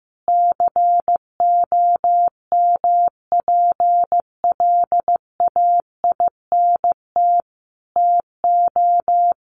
Morse code, heard as a keyed tone, about 15 wpm, 700 Hz. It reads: COMPLAINT TO
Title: Sentences from Top 1400 Words 15wpm